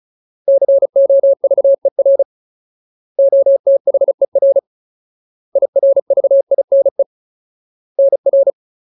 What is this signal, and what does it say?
Morse code, heard as a keyed tone, about 34 wpm, 550 Hz. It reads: COVER OTHER IRVINE NR
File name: Ultimate Word List / 35WPM Ultimate